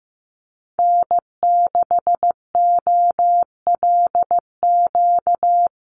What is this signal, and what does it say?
Morse code, heard as a keyed tone, about 15 wpm, 700 Hz. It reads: N6OLQ
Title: USA Call Signs Repeated 3x 15wpm